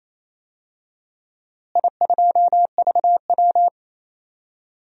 Morse code, heard as a keyed tone, about 28 wpm, 700 Hz. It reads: I2VW
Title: DXCC-100 Call Signs - 4 Characters - Repeated 2x - 28wpm